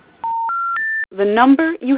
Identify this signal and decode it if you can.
Special information tone, with speech.